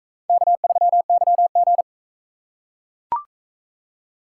Morse code, heard as a keyed tone, about 42 wpm, 700 Hz. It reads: K3YG